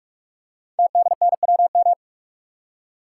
Morse code, heard as a keyed tone, about 44 wpm, 700 Hz. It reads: TDNWM
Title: Code Groups - 5 Letters - 45wpm